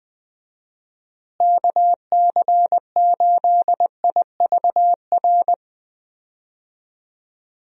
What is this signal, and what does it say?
Morse code, heard as a keyed tone, about 20 wpm, 700 Hz. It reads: KC8IVR
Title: USA Call Signs - 6 Characters - Repeated 2x - 20wpm